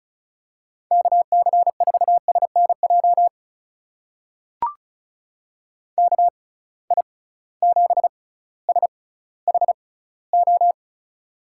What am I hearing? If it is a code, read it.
Morse code, heard as a keyed tone, about 36 wpm, spaced as for 18 wpm, 700 Hz. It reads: KC4SNJ KI7SHO
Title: USA Call Signs - 6 Characters - 6x Character Spacing - 35wpm